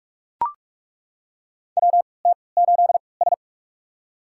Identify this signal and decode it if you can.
Morse code, heard as a keyed tone, about 41 wpm, spaced as for 25 wpm, 690 Hz. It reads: WT8S
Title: USA Call Signs - 4 Characters - 3x Character Spacing - 45wpm